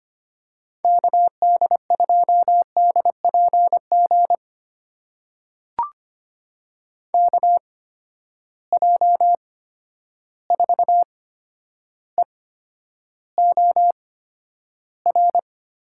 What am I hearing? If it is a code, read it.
Morse code, heard as a keyed tone, about 25 wpm, spaced as for 11 wpm, 700 Hz. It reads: KD2DPG KJ4EOR
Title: USA Call Signs - 6 Characters - 8x Character Spacing - 25wpm